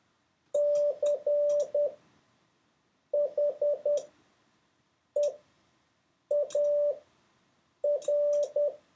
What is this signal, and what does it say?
Morse code, heard as a keyed tone, about 10 wpm, spaced as for 6 wpm, 580 Hz. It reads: CHEAR